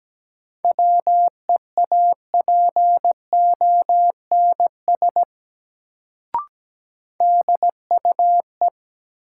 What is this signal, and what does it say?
Morse code, heard as a keyed tone, about 17 wpm, 700 Hz. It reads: WEAPONS DUE
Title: Sets of 5 Words Encore - New Word Drill - 17wpm